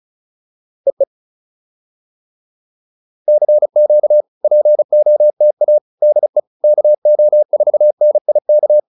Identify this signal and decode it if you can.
Morse code, heard as a keyed tone, about 36 wpm, 600 Hz. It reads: EE CQ POTA DE KO4NIK